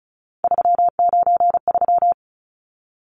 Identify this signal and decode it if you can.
Morse code, heard as a keyed tone, about 34 wpm, 700 Hz. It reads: 393
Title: Numbers - 3-Digit 35wpm